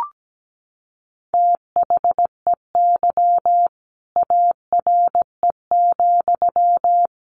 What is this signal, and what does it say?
Morse code, heard as a keyed tone, about 17 wpm, 700 Hz. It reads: THEY ARE,
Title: Sentences from Top 1200 Words 17wpm